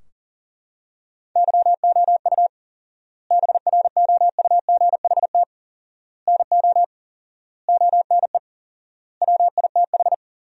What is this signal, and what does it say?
Morse code, heard as a keyed tone, about 40 wpm, 700 Hz. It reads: YOU BROUGHT NO ONE WITH
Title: 4x Word-Spacing - Sentences from Top 500 Words - 40wpm